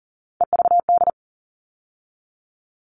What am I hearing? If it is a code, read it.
Morse code, heard as a keyed tone, about 40 wpm, 700 Hz. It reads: EVD
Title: All Cut Numbers - 3-Digit 40wpm